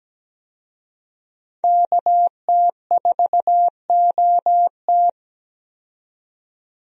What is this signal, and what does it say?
Morse code, heard as a keyed tone, about 17 wpm, 700 Hz. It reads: KT4OT
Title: USA Call Signs - 5 Characters - 17wpm